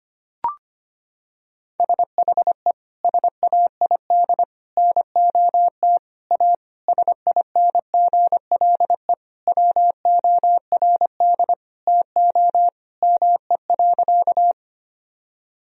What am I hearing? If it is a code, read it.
Morse code, heard as a keyed tone, about 25 wpm, 700 Hz. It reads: SHE SAID NOT A SINGLE WORD TO ME.
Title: Sentences from Top 1000 Words - Review Part 4 - 25wpm